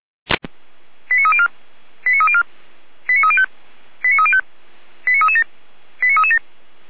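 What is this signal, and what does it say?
Five-tone selective call (ZVEI), heard as quick runs of five tones, tones of about 70 ms. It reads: 89304 89304 89305 89305 89337 89337
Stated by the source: Five-tone ZVEI selective call 89174 received on an SDR.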